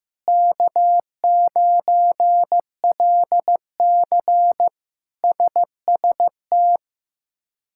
Morse code, heard as a keyed tone, about 15 wpm, 700 Hz. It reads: K9LC SST
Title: K1USN SST QSO Practice - 15wpm